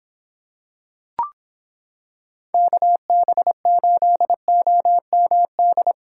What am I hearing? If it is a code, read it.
Morse code, heard as a keyed tone, about 26 wpm, 700 Hz. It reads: KB8OMD